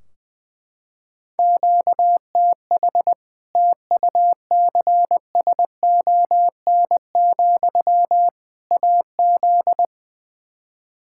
Morse code, heard as a keyed tone, about 21 wpm, 700 Hz. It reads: QTH TUCSON, AZ
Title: Practice QSO - Exchange 1 - Beginning - 20wpm